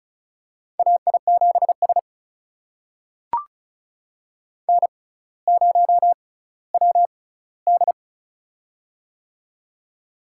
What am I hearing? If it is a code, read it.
Morse code, heard as a keyed tone, about 36 wpm, spaced as for 18 wpm, 700 Hz. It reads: AI7S N0WD
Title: USA Call Signs - 4 Characters - 6x Character Spacing - 35wpm